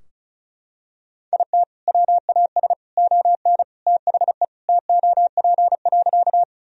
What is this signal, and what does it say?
Morse code, heard as a keyed tone, about 36 wpm, 700 Hz. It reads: IT WAS ON THE TOP.
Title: Sentences from Top 1000 Words - Review Part 2 - 35wpm